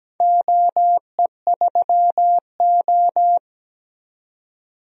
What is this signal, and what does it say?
Morse code, heard as a keyed tone, about 17 wpm, 700 Hz. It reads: OE3O
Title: DXCC-100 Call Signs - 4 Characters - 17wpm